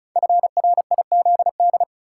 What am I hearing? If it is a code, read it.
Morse code, heard as a keyed tone, about 36 wpm, 700 Hz. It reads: FRIZD
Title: Code Groups - 5 Letter:Numbers - 35wpm